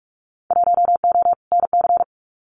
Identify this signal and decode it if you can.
Morse code, heard as a keyed tone, about 44 wpm, 700 Hz. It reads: 1O NC
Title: Winter Field Day Exchanges 45wpm